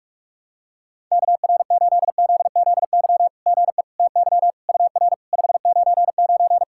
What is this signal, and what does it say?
Morse code, heard as a keyed tone, about 44 wpm, 690 Hz. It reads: KR8ZZY GE TY UR 599
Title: SOTA QSO Practice 45wpm